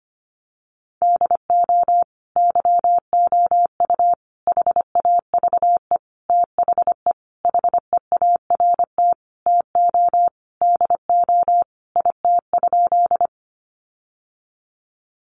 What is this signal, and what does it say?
Morse code, heard as a keyed tone, about 25 wpm, 700 Hz. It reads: DO YOU HAVE THE HEART TO DO IT?
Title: Sentences from Top 1200 Words 25wpm